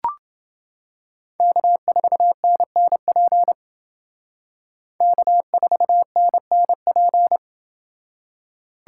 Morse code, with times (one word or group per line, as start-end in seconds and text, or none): 1.4-3.5 K4NNP
5.0-7.4 K4NNP